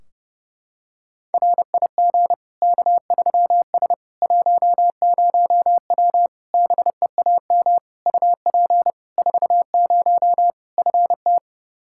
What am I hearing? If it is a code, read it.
Morse code, heard as a keyed tone, about 32 wpm, 700 Hz. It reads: RIG K3S 10W BEAM UP 40 FT
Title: Practice QSO - Exchange 2 - Info - 30wpm